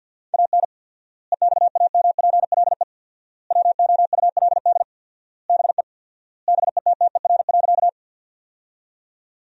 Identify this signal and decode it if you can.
Morse code, heard as a keyed tone, about 50 wpm, 700 Hz. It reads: AN EXAMPLE WOULD BE BETTER.
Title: 4x Word-Spacing - Sentences from Top 400 Words - 50wpm